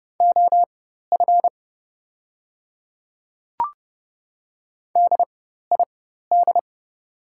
Morse code, heard as a keyed tone, about 30 wpm, spaced as for 14 wpm, 700 Hz. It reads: OF DID
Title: 4x Character Spacing - Top 100 Words - 30wpm